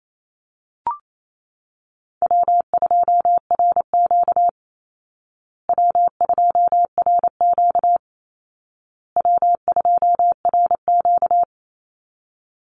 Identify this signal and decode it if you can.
Morse code, heard as a keyed tone, about 28 wpm, 700 Hz. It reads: W2RQ W2RQ W2RQ
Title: Easy Call Signs Repeated 3x - 28wpm